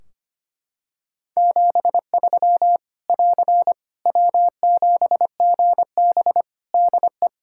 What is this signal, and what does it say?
Morse code, heard as a keyed tone, about 25 wpm, 700 Hz. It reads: 73 <AR> W7GB DE
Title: Practice QSO - Exchange 4 - Ending - 25wpm